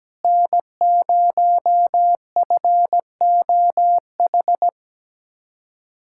Morse code, heard as a keyed tone, about 17 wpm, 700 Hz. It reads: N0FOH
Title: USA Callsign DE Callsign - 5 Characters - 17wpm